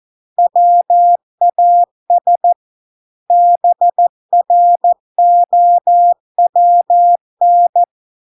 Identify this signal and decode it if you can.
Morse code, heard as a keyed tone, about 14 wpm, 700 Hz. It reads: WAS BROWN